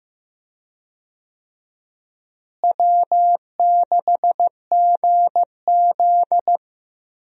Morse code, heard as a keyed tone, about 15 wpm, 700 Hz. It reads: W6GZ
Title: USA Call Signs - 4 Characters - Repeated 2x - 15wpm